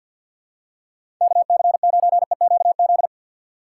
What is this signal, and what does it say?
Morse code, heard as a keyed tone, about 53 wpm, 700 Hz. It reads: KC9EQZ
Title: USA Call Signs - 6 Characters - 4x Character Spacing - 50wpm